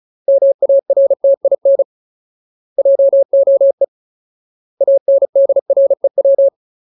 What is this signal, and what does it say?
Morse code, heard as a keyed tone, about 37 wpm, 550 Hz. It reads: MARTIN JOE ANDREW
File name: Ultimate Word List / 35WPM Ultimate